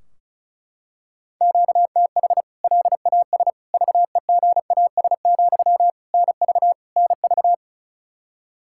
Morse code, heard as a keyed tone, about 34 wpm, 700 Hz. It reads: QTH LAS VEGAS, NV NV
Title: K1USN SST QSO Practice 35wpm